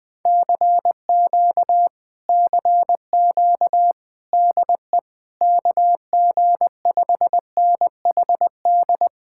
Morse code, heard as a keyed tone, about 20 wpm, 700 Hz. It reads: CQ CQ DE KG5NHD
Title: Speed-Racing Common QSO Phrases - 17wpm